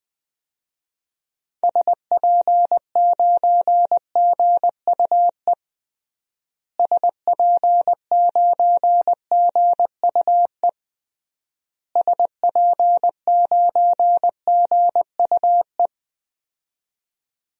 Morse code, heard as a keyed tone, about 21 wpm, 700 Hz. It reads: SP9GUE SP9GUE SP9GUE
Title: DXCC-100 Call Signs - 6 Characters - Repeated 3x - 20wpm